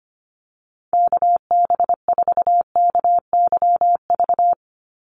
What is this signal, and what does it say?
Morse code, heard as a keyed tone, about 25 wpm, 700 Hz. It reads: KB4KYV
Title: USA Call Signs 25wpm